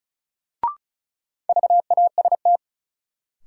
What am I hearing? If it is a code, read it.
Morse code, heard as a keyed tone, about 36 wpm, 700 Hz. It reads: VAST
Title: Sentences from Top 1300 Words - New Word Drill - 35wpm